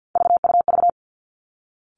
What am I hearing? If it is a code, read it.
Morse code, heard as a keyed tone, about 50 wpm, 700 Hz. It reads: VUV
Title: Three Challenging Letter-Number Combinations 50wpm